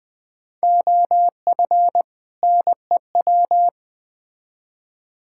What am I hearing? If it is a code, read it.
Morse code, heard as a keyed tone, about 20 wpm, 700 Hz. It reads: OF NEW
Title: Build A Phrase - 3-Word Phrases - 20wpm